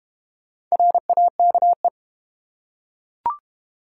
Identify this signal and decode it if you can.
Morse code, heard as a keyed tone, about 32 wpm, 700 Hz. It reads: RAKE